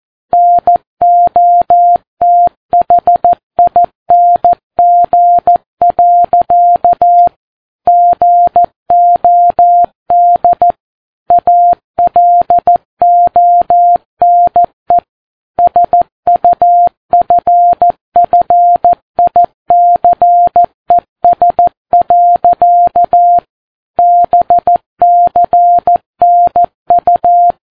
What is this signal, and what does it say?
Morse code, heard as a keyed tone, about 14 wpm, 700 Hz. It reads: NOTHING. GOD ALONE SUFFICES. BCNU
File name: Ham Radio Sat